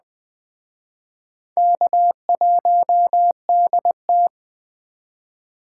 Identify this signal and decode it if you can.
Morse code, heard as a keyed tone, about 20 wpm, 700 Hz. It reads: K1DT
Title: Speed Racing - USA 1x2 Call Signs 17wpm